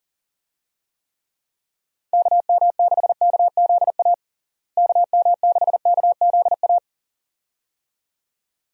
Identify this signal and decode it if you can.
Morse code, heard as a keyed tone, about 40 wpm, 700 Hz. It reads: KM6KZA KM6KZA
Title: USA Call Signs - 6 Characters - Repeated 2x - 40wpm